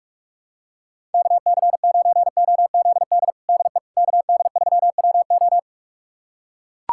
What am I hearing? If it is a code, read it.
Morse code, heard as a keyed tone, about 41 wpm, 700 Hz. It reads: KC9OZD DE KD3WO
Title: USA Callsign DE Callsign 45wpm